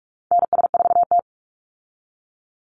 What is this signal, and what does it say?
Morse code, heard as a keyed tone, about 44 wpm, 700 Hz. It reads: NS4T
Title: Call Signs Repeated 3x - 45wpm